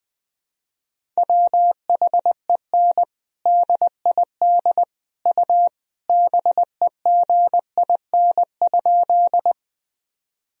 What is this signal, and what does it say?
Morse code, heard as a keyed tone, about 20 wpm, 700 Hz. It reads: WHEN DID U BEGIN?
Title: Sentences from Top 300 Words and Abbreviations 20wpm